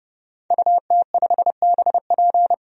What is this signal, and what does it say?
Morse code, heard as a keyed tone, about 31 wpm, 700 Hz. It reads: UT5BP
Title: DXCC-100 Call Signs - 5 Characters - 30wpm